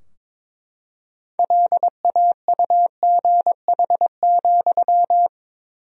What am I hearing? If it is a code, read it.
Morse code, heard as a keyed tone, about 21 wpm, 700 Hz. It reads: LAUGH,
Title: 4x Word-Spacing - Sentences from Top 500 Words - 22wpm